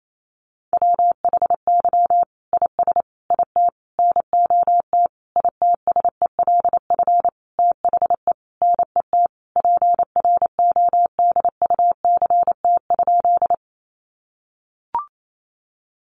Morse code, heard as a keyed tone, about 28 wpm, 700 Hz. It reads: WHY IS IT NOT ITSELF THE NET PRODUCT?
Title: Sentences from Top 1400 Words 28wpm